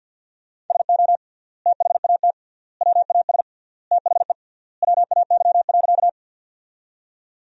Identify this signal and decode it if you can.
Morse code, heard as a keyed tone, about 50 wpm, 700 Hz. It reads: SO THAT WAS THE WAY.
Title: 3x Word-Spacing - Sentences from Top 100 Words - 50wpm